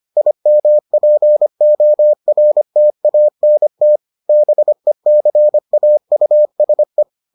Morse code, heard as a keyed tone, about 25 wpm, 600 Hz. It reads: IMPORTANT BECAUSE